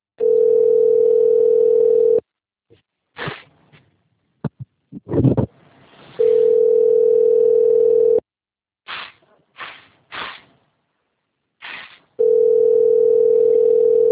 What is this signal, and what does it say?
Ringback tone (North America), 440 + 480 Hz.